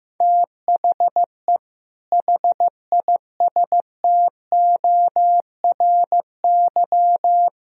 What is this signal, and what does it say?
Morse code, heard as a keyed tone, about 15 wpm, 700 Hz. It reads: THE HISTORY
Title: Speed-Racing - Sets of 3 Words 15wpm